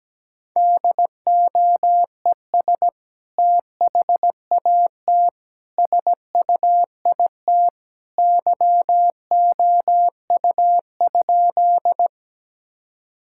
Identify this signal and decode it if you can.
Morse code, heard as a keyed tone, about 17 wpm, 700 Hz. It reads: DOES THAT SUIT YOU?
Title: Sentences from Top 1600 Words 17wpm